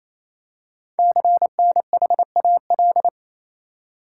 Morse code, heard as a keyed tone, about 27 wpm, 700 Hz. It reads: CNHAL
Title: Code Groups - 5 Letters - 28wpm